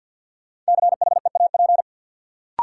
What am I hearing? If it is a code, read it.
Morse code, heard as a keyed tone, about 53 wpm, 700 Hz. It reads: CHEAP
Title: Sentences from Top 1200 Words - New Word Drill - 50wpm